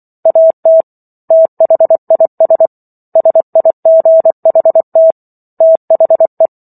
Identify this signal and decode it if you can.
Morse code, heard as a keyed tone, about 24 wpm, 650 Hz. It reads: AT THIS SIGHT THE